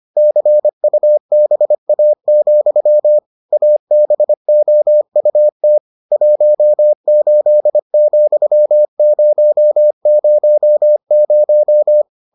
Morse code, heard as a keyed tone, about 25 wpm, 600 Hz. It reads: CUBA, ABOUT 18,000